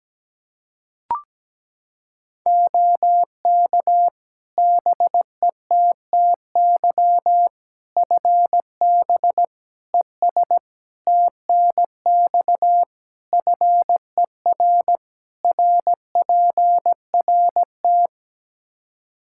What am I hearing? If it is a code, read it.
Morse code, heard as a keyed tone, about 17 wpm, 700 Hz. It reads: OK BETTY FB ES TNX FER RPRT
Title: Practice QSO - Exchange 2 - Info - 17wpm